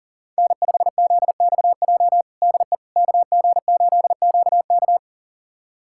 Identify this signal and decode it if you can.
Morse code, heard as a keyed tone, about 40 wpm, 700 Hz. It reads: N5ZXJ DE KG8QK
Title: USA Callsign DE Callsign - 5 Characters - 40wpm